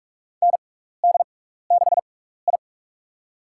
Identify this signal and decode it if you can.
Morse code, heard as a keyed tone, about 44 wpm, spaced as for 15 wpm, 700 Hz. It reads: ND6I
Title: USA Call Signs - 4 Characters - 6x Character Spacing - 45wpm